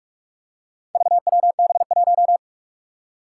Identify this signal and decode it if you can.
Morse code, heard as a keyed tone, about 44 wpm, 700 Hz. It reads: VWB1
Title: Four Challenging Letter-Number Combinations 45wpm